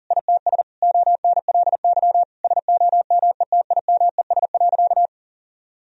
Morse code, heard as a keyed tone, about 37 wpm, 700 Hz. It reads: ITS ONLY SOMETIMES.
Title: Sentences from Top 1000 Words - Review Part 3 - 40wpm